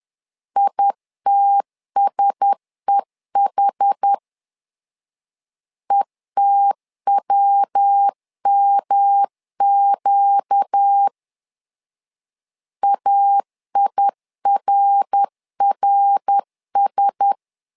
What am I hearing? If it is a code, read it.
Morse code, heard as a keyed tone, about 10 wpm, 800 Hz. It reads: ITSEH ETWMQ AIRRS
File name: LEZIONI / 4 Q1S 10wpm 2133